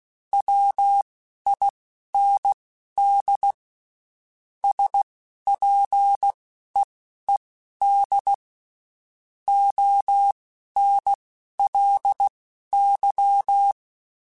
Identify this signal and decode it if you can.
Morse code, heard as a keyed tone, about 16 wpm, spaced as for 11 wpm, 800 Hz. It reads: WIND SPEED ONLY
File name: Spaced Characters / Spaced Characters-mp3s / Weather 5 1 sp @15